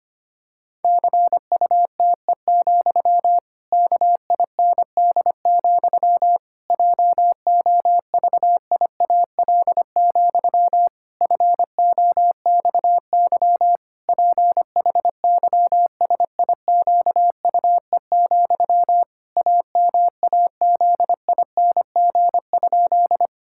Morse code, heard as a keyed tone, about 25 wpm, 700 Hz. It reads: CUTE, KIND, JOVIAL, FOXY PHYSIQUE, AMAZING?
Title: Pangram Challenge v2 - 25wpm